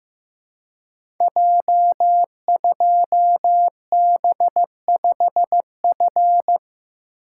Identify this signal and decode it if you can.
Morse code, heard as a keyed tone, about 15 wpm, 700 Hz. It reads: J2B5F
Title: Code Groups - 5 Letter:Numbers - 15wpm